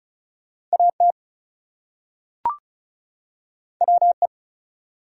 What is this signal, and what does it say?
Morse code, heard as a keyed tone, about 36 wpm, 700 Hz. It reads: AT WE
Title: Two Letter Words - 35wpm